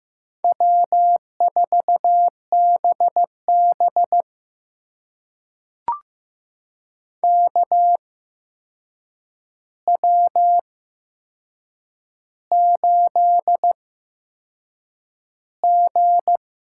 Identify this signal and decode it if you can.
Morse code, heard as a keyed tone, about 15 wpm, spaced as for 6 wpm, 700 Hz. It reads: W4BB KW8G
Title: USA Call Signs - 4 Characters - 8x Character Spacing - 15wpm